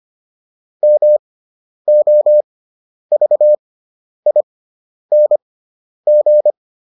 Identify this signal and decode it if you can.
Morse code, heard as a keyed tone, about 25 wpm, spaced as for 10 wpm, 600 Hz. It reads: MOVING